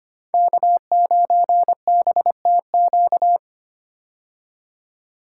Morse code, heard as a keyed tone, about 25 wpm, 700 Hz. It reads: K9BTQ
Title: K1USN SST QSO Practice - 25wpm